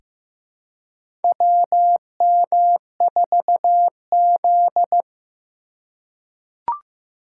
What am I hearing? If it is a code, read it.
Morse code, heard as a keyed tone, about 15 wpm, 700 Hz. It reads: WM4Z
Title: USA Call Signs - 4 Characters - 4x Character Spacing - 15wpm